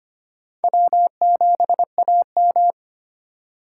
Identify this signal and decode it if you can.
Morse code, heard as a keyed tone, about 25 wpm, 700 Hz. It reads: W7AM
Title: USA Call Signs - 3x Character Spacing - 25wpm